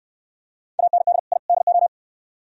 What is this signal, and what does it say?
Morse code, heard as a keyed tone, about 68 wpm, 700 Hz. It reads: AND I DO